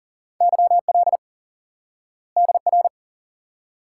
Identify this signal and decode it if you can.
Morse code, heard as a keyed tone, about 40 wpm, 700 Hz. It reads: YL DR
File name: IQE Course 3 - Lesson 89 - YAGI - Two Elements - 40wpm